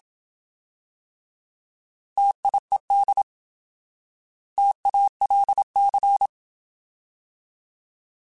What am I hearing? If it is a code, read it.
Morse code, heard as a keyed tone, about 27 wpm, 800 Hz. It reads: TIED TALC
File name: Alphabetical / Alphabetical mp3s / Ts @25